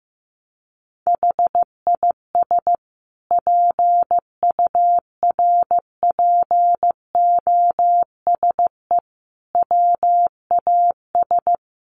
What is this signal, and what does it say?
Morse code, heard as a keyed tone, about 15 wpm, 700 Hz. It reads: HIS PURPOSE WAS
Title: Sentences from Top 1100 Words 15wpm